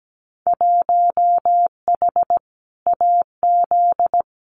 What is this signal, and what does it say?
Morse code, heard as a keyed tone, about 17 wpm, 700 Hz. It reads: 1H AZ
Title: Winter Field Day Exchanges 17wpm